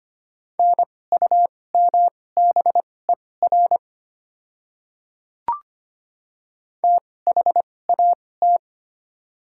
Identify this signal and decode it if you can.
Morse code, heard as a keyed tone, about 25 wpm, spaced as for 18 wpm, 700 Hz. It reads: NUMBER THAT
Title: 2x Character Spacing - Top 100 Words - 25wpm